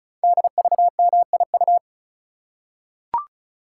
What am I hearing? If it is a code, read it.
Morse code, heard as a keyed tone, about 36 wpm, 700 Hz. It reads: DVMIU
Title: Code Groups - 5 Letters - 35wpm